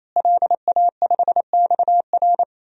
Morse code, heard as a keyed tone, about 27 wpm, 700 Hz. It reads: LA5XR
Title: DXCC-100 Call Signs - 5 Characters - Repeated 3x - 28wpm